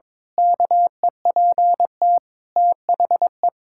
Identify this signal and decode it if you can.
Morse code, heard as a keyed tone, about 22 wpm, 700 Hz. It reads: KEPT THE
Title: Pangram Challenge v2 - 22wpm